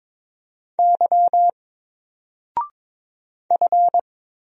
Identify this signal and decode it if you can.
Morse code, heard as a keyed tone, about 22 wpm, 700 Hz. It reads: Y F
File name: Single-Letter-Number-22wpm